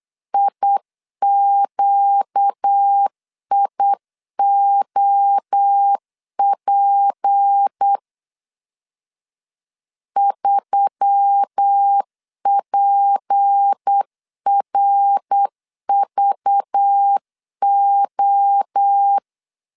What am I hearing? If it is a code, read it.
Morse code, heard as a keyed tone, about 8 wpm, 800 Hz. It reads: IQIOP 3PRVO
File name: CW / DET / 1 6 1500C S C 8wpm 5219